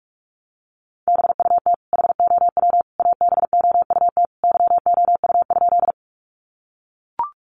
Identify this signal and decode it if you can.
Morse code, heard as a keyed tone, about 44 wpm, 700 Hz. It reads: BUT HOW ABOUT YOU?
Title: Sentences from Top 100 Words 45wpm